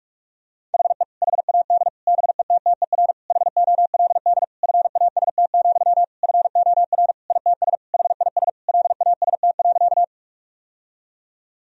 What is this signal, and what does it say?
Morse code, heard as a keyed tone, about 46 wpm, 700 Hz. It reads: HE HAD BETTER HOLD FAST, FOR ITS HIS LAST.
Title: Sentences from Top 1000 Words - Review Part 2 - 45wpm